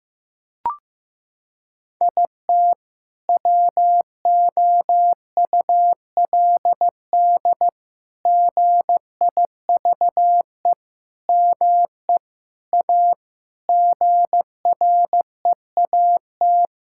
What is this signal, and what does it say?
Morse code, heard as a keyed tone, about 16 wpm, 700 Hz. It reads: IT WOULD GIVE ME A GREAT
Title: Sentences from Top 1600 Words 15wpm